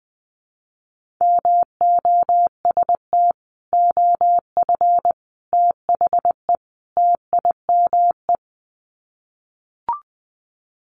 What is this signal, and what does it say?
Morse code, heard as a keyed tone, about 20 wpm, 700 Hz. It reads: MOST OF THE TIME
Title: Sets of 4 Words 20wpm